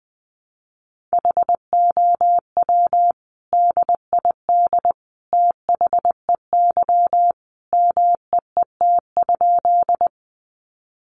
Morse code, heard as a keyed tone, about 20 wpm, 700 Hz. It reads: HOW DID THEY MEET?
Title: Sentences from Top 1100 Words 20wpm